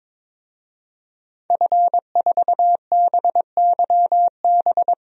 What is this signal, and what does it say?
Morse code, heard as a keyed tone, about 21 wpm, 700 Hz. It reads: F4BYB
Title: DXCC-100 Call Signs - 5 Characters - 22wpm